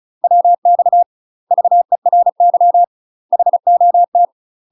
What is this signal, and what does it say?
Morse code, heard as a keyed tone, about 34 wpm, 700 Hz. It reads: WX VERY HOT
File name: UK MASTER QSOs TRAINING / 35WPM / UK MASTER QSOs 0034